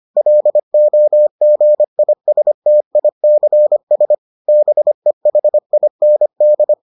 Morse code, heard as a keyed tone, about 25 wpm, 600 Hz. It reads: LOGISTICS BEHIND